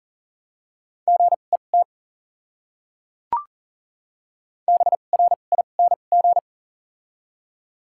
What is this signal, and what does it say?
Morse code, heard as a keyed tone, about 40 wpm, spaced as for 29 wpm, 700 Hz. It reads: GET BRING
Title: 2x Character Spacing - Top 500 Words - 40wpm